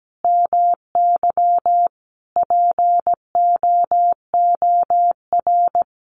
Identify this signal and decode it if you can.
Morse code, heard as a keyed tone, about 17 wpm, 700 Hz. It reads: MY POOR
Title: Sentences from Top 1000 Words 17wpm